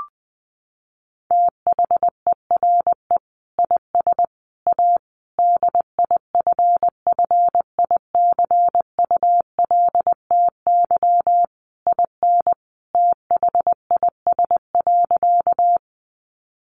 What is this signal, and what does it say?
Morse code, heard as a keyed tone, about 20 wpm, 700 Hz. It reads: THERE IS A DIFFICULTY IN THIS.
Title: Sentences from Top 1500 Words 20wpm